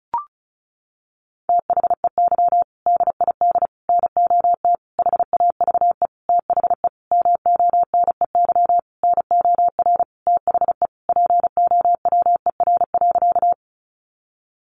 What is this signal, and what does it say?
Morse code, heard as a keyed tone, about 34 wpm, 700 Hz. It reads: THEY DID NOT HAVE THE MONEY NOR THE POWER.
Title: Sentences from Top 1500 Words 35wpm